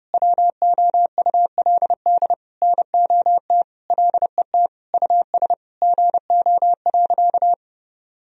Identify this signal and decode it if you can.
Morse code, heard as a keyed tone, about 30 wpm, 700 Hz. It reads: WOULD NOT LET US GO.
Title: Sentences from Top 1000 Words - Review Part 2 - 30wpm